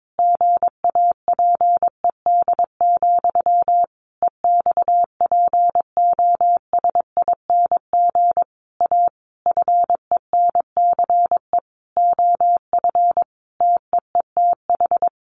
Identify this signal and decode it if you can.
Morse code, heard as a keyed tone, about 22 wpm, 700 Hz. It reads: GAPED, EXPOSING A FENCE OF TEETH
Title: A Sound Of Thunder 22wpm